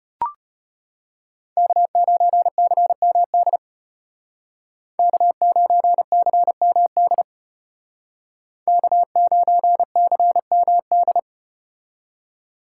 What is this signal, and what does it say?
Morse code, heard as a keyed tone, about 33 wpm, 700 Hz. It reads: K9CMD K9CMD K9CMD